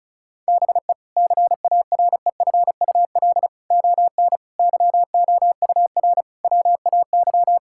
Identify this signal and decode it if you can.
Morse code, heard as a keyed tone, about 36 wpm, 700 Hz. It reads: BE CAREFUL ON YOUR WAY
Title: Sentences from Top 1600 Words 35wpm